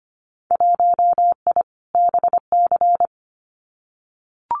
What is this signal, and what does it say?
Morse code, heard as a keyed tone, about 25 wpm, 700 Hz. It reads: 1I BC
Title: Winter Field Day Exchanges Encore 25wpm